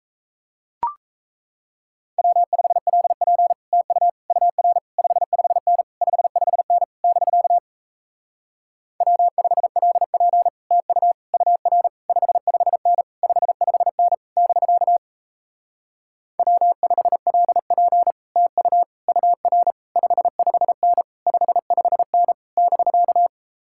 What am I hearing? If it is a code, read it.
Morse code, heard as a keyed tone, about 38 wpm, 700 Hz. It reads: W5LP TU UR 55N 55N <BK> W5LP TU UR 55N 55N <BK> W5LP TU UR 55N 55N <BK>